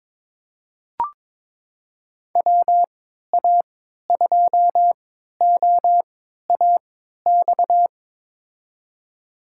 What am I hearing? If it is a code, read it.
Morse code, heard as a keyed tone, about 22 wpm, spaced as for 12 wpm, 700 Hz. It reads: WA2OAX
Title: USA Call Signs - 6 Characters - 3x Character Spacing - 22wpm